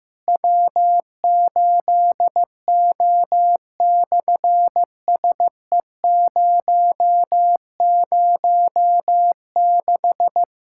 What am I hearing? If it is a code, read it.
Morse code, heard as a keyed tone, about 15 wpm, 700 Hz. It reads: W8O/SE006
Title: SOTA QSO Practice 15wpm